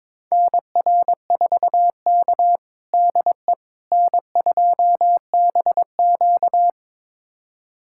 Morse code, heard as a keyed tone, about 22 wpm, 700 Hz. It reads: NR4K DE N2BQ
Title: USA Callsign DE Callsign - 4 Characters - 22wpm